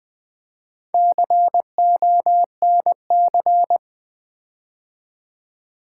Morse code, heard as a keyed tone, about 20 wpm, 700 Hz. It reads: CONC
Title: Build A Word - 4 Letters At A Time - 11-Letter Words - 20wpm